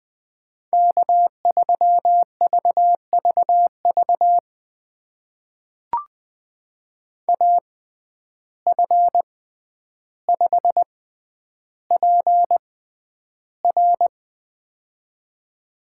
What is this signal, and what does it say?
Morse code, heard as a keyed tone, about 20 wpm, spaced as for 10 wpm, 700 Hz. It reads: K3VVV AF5PR
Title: USA Call Signs - 5 Characters - 6x Character Spacing - 20wpm